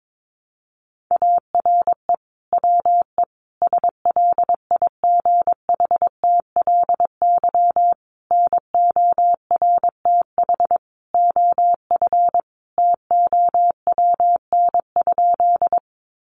Morse code, heard as a keyed tone, about 22 wpm, 700 Hz. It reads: ARE WE SLIGHTLY NORTH OF TOWN?
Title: Sentences from Top 1300 Words 22wpm